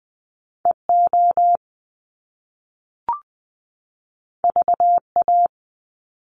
Morse code, heard as a keyed tone, about 20 wpm, 700 Hz. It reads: E O V A
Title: Top Two Letter Combinations in Words 20wpm